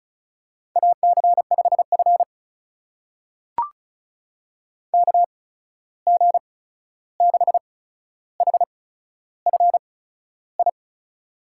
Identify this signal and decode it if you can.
Morse code, heard as a keyed tone, about 36 wpm, spaced as for 13 wpm, 700 Hz. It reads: AC5F KG6HFI
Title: USA Call Signs - 8x Character Spacing - 35wpm